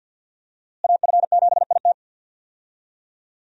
Morse code, heard as a keyed tone, about 51 wpm, 700 Hz. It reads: AF7IT
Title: USA Call Signs - 5 Characters - Repeated 3x - 50wpm